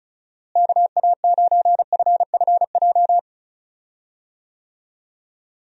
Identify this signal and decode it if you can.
Morse code, heard as a keyed tone, about 34 wpm, 700 Hz. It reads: KA9FFJ
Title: USA Call Signs - 6 Characters - Repeated 3x - 35wpm